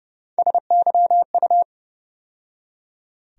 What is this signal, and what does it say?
Morse code, heard as a keyed tone, about 30 wpm, 700 Hz. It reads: SYU
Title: USA Call Sign Postfixes - 3 Letters 30wpm